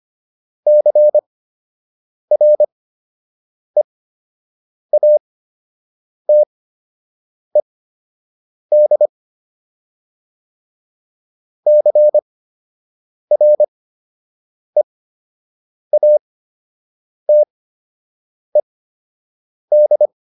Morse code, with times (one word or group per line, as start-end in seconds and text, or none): 0.7-9.1 CREATED
11.7-20.1 CREATED